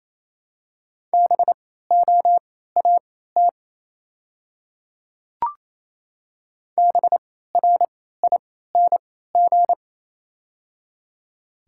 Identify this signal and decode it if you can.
Morse code, heard as a keyed tone, about 28 wpm, spaced as for 16 wpm, 700 Hz. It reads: BOAT BRING
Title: 3x Character Spacing - Top 500 Words - 28wpm